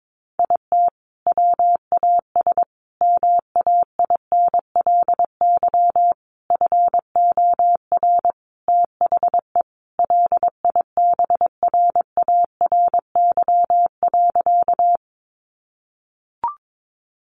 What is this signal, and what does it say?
Morse code, heard as a keyed tone, about 22 wpm, 700 Hz. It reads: IT WAS MAINLY FOR THE LIBRARY.
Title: Sentences from Top 1400 Words 22wpm